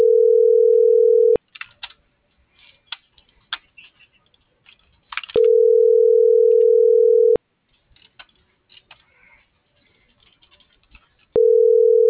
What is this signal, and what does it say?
Ringback tone (North America), 440 + 480 Hz.